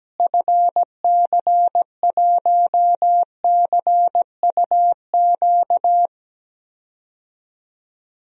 Morse code, heard as a keyed tone, about 17 wpm, 700 Hz. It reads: FC1CUQ
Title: DXCC-100 Call Signs - 6 Characters - Repeated 3x - 17wpm